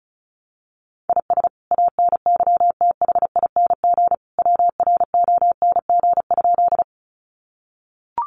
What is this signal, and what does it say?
Morse code, heard as a keyed tone, about 34 wpm, 700 Hz. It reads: IS ANYTHING WRONG?
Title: Sentences from Top 1000 Words 35wpm